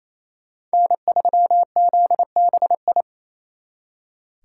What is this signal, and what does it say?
Morse code, heard as a keyed tone, about 29 wpm, 700 Hz. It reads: N3ZBI
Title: Speed Racing - USA 1x3 Call Signs 28wpm